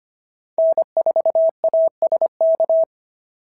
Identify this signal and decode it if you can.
Morse code, heard as a keyed tone, about 25 wpm, 650 Hz. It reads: N4ASK
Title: SOTA QSO Practice 25wpm